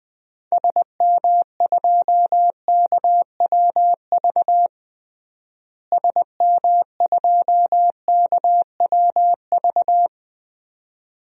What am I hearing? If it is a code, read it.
Morse code, heard as a keyed tone, about 20 wpm, 700 Hz. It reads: SM2KWV SM2KWV
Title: DXCC-100 Call Signs - 6 Characters - Repeated 2x - 20wpm